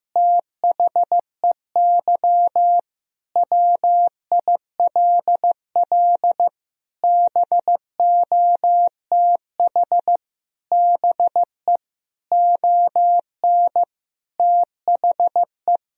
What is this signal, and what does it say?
Morse code, heard as a keyed tone, about 15 wpm, 700 Hz. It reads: THEY WILL BOTH BE ON THE
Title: Sentences from Top 1000 Words - Review Part 2 - 15wpm